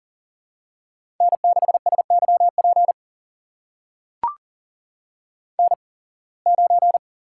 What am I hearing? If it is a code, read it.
Morse code, heard as a keyed tone, about 40 wpm, spaced as for 26 wpm, 700 Hz. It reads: N6SYP N9
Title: USA Call Signs - 8x Character Spacing - 40wpm